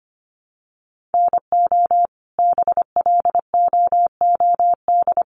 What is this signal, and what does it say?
Morse code, heard as a keyed tone, about 25 wpm, 700 Hz. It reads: NO BLOOD
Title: Sentences from Top 1200 Words 25wpm